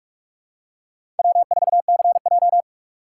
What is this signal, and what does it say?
Morse code, heard as a keyed tone, about 46 wpm, 700 Hz. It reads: W4CJ
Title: USA Call Signs - 4 Characters - 3x Character Spacing - 45wpm